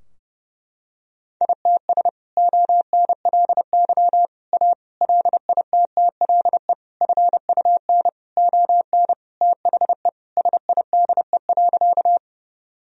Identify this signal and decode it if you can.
Morse code, heard as a keyed tone, about 32 wpm, 700 Hz. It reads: ITS ONLY A LITTLE FUN ON THE SIDE.
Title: Sentences from Top 1000 Words - Review Part 4 - 30wpm